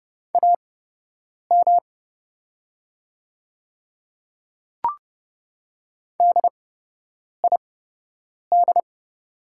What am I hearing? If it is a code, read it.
Morse code, heard as a keyed tone, about 29 wpm, spaced as for 8 wpm, 700 Hz. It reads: AM DID
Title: 8x Character Spacing - Top 500 Words - 30wpm